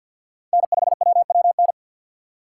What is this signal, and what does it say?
Morse code, heard as a keyed tone, about 62 wpm, 700 Hz. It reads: N5WWN
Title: Speed Racing - USA Call Signs 50wpm